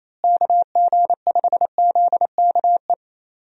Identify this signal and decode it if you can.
Morse code, heard as a keyed tone, about 28 wpm, 700 Hz. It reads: KG5ZKE
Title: USA Call Signs - 6 Characters - Repeated 2x - 28wpm